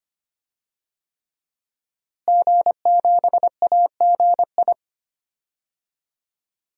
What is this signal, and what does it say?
Morse code, heard as a keyed tone, about 25 wpm, 700 Hz. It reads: G7AGI
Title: DXCC-100 Call Signs - 5 Characters - Repeated 3x - 25wpm